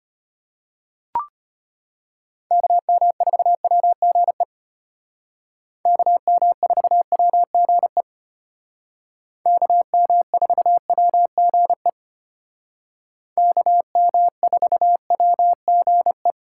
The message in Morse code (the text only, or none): KM4WGE KM4WGE KM4WGE KM4WGE